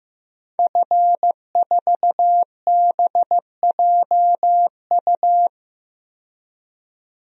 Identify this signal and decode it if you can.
Morse code, heard as a keyed tone, about 15 wpm, 700 Hz. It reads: F4BJU
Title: DXCC-100 Call Signs - 5 Characters - 15wpm